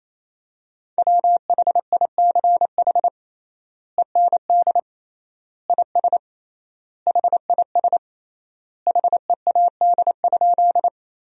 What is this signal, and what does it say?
Morse code, heard as a keyed tone, about 28 wpm, 700 Hz. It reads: WHICH END IS HIS HEAD?
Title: 3x Word-Spacing - Sentences from Top 300 Words - 28wpm